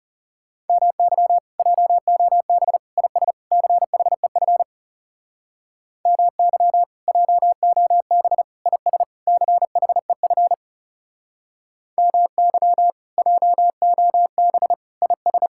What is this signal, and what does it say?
Morse code, heard as a keyed tone, about 37 wpm, 700 Hz. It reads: MY JOB IS CHEF MY JOB IS CHEF MY JOB IS
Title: Speed-Racing Common QSO Phrases Encore - 30wpm